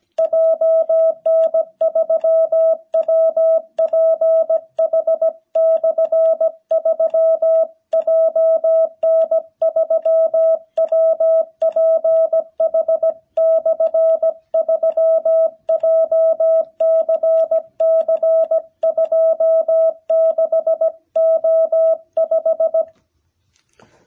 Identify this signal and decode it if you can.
Morse code, heard as a keyed tone, about 17 wpm, 660 Hz. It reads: JN3WPH/3 JN3WPH/3 JCC26O5
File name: jh8cbh / 2021cw